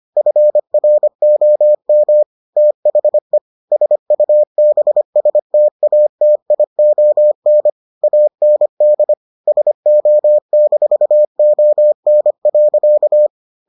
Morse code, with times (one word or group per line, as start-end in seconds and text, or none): 0.2-2.2 FROM
2.6-3.4 THE
3.7-7.7 SUBSTATION
8.0-9.1 AND
9.5-13.3 SO-ON.